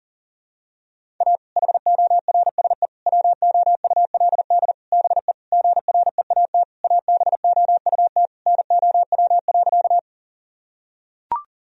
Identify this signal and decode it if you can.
Morse code, heard as a keyed tone, about 40 wpm, 700 Hz. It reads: A HORSE WOULD BE GREAT ABOUT NOW.
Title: Sentences from Top 1000 Words - Review Part 1 - 40wpm